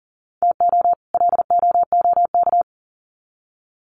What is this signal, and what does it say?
Morse code, heard as a keyed tone, about 40 wpm, 700 Hz. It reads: TO LOOK
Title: Sets of 2 Words 40wpm